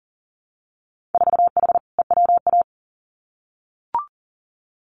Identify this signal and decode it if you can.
Morse code, heard as a keyed tone, about 40 wpm, 700 Hz. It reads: 4H EWA
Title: Winter Field Day Exchanges Encore 40wpm